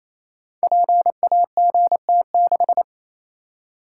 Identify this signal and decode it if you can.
Morse code, heard as a keyed tone, about 27 wpm, 700 Hz. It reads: PAGT6
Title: Code Groups - 5 Letter:Numbers - 28wpm